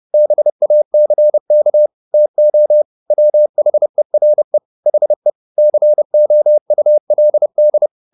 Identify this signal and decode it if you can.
Morse code, heard as a keyed tone, about 30 wpm, 600 Hz. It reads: BACK TO WHERE HE COULD